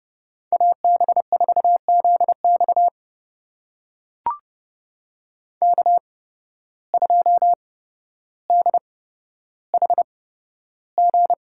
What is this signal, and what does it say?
Morse code, heard as a keyed tone, about 30 wpm, spaced as for 13 wpm, 700 Hz. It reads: AB4ZX K2DHG
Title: USA Call Signs - 5 Characters - 8x Character Spacing - 30wpm